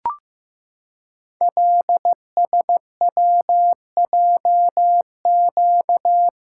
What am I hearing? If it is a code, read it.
Morse code, heard as a keyed tone, about 15 wpm, 700 Hz. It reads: LSWJQ
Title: Code Groups - 5 Letter:Numbers - 15wpm